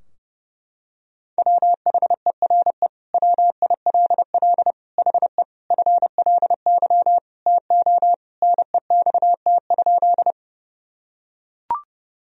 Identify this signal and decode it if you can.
Morse code, heard as a keyed tone, about 30 wpm, 700 Hz. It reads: WHERE WILL HE FLY TO NEXT?
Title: Sentences from Top 1000 Words - Review Part 2 - 30wpm